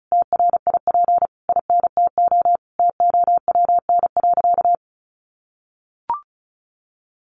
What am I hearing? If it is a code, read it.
Morse code, heard as a keyed tone, about 34 wpm, 700 Hz. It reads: TRIP INTO TOWN.
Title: Sentences from Top 1300 Words 35wpm